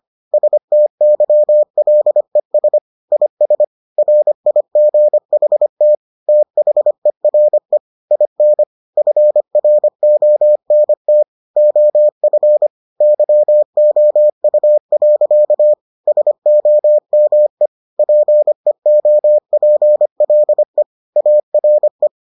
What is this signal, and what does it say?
Morse code, heard as a keyed tone, about 25 wpm, 600 Hz. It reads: STYLES IS RIGHT THERE IN FRONT OF YOU. SOME PEOPLE ARE